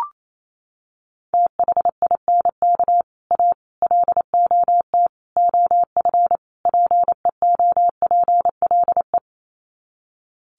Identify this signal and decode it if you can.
Morse code, heard as a keyed tone, about 28 wpm, 700 Hz. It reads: THINK A LOT OF PEOPLE
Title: Sets of 5 Words 28wpm